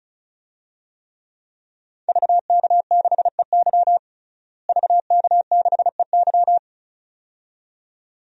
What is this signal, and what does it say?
Morse code, heard as a keyed tone, about 36 wpm, 700 Hz. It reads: VK6EY VK6EY
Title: DXCC-100 Call Signs - 5 Characters - Repeated 2x - 35wpm